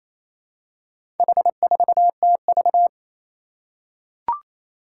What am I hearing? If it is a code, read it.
Morse code, heard as a keyed tone, about 28 wpm, 700 Hz. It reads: H4TV
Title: Four Challenging Letter-Number Combinations 28wpm